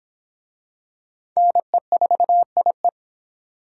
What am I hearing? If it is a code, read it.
Morse code, heard as a keyed tone, about 24 wpm, 700 Hz. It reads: NE4IE